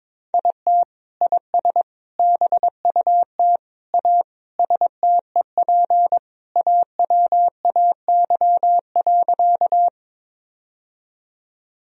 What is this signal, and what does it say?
Morse code, heard as a keyed tone, about 22 wpm, 700 Hz. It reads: IT IS BUT A STEP AWAY.
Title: Sentences from Top 1000 Words - Review Part 2 - 22wpm